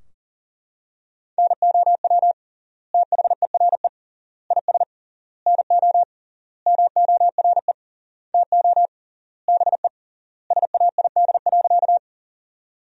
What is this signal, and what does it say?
Morse code, heard as a keyed tone, about 40 wpm, 700 Hz. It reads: NOW THERE IS NO MORE TO BE SAID.
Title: 3x Word-Spacing - Sentences from Top 100 Words - 40wpm